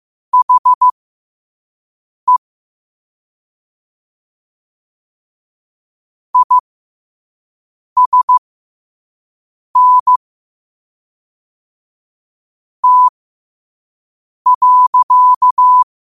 Morse code, heard as a keyed tone, about 15 wpm, spaced as for 5 wpm, 1000 Hz. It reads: HE ISN T.
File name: qotdfiles / 5wpm